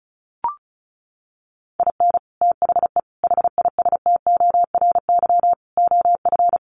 Morse code, heard as a keyed tone, about 34 wpm, 700 Hz. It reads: IN THE HISTORY OF
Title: Sets of 4 Words 35wpm